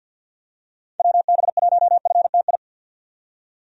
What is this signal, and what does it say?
Morse code, heard as a keyed tone, about 46 wpm, 700 Hz. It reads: WB1FTI
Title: USA Call Signs - 6 Characters - 50wpm